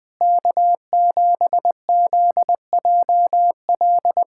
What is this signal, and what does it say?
Morse code, heard as a keyed tone, about 21 wpm, 700 Hz. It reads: K7ZJL
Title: USA Callsign DE Callsign - 5 Characters - 20wpm